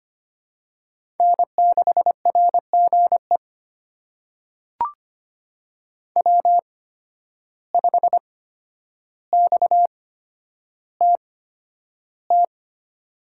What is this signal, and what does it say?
Morse code, heard as a keyed tone, about 25 wpm, spaced as for 11 wpm, 700 Hz. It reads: N6RGE W5XTT
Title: USA Call Signs - 5 Characters - 8x Character Spacing - 25wpm